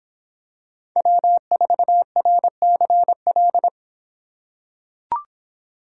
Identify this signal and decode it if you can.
Morse code, heard as a keyed tone, about 26 wpm, 700 Hz. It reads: W4RCL